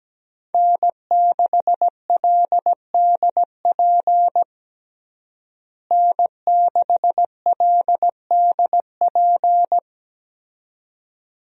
Morse code, heard as a keyed tone, about 17 wpm, 700 Hz. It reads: N6LDP N6LDP
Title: USA Call Signs - 5 Characters - Repeated 2x - 17wpm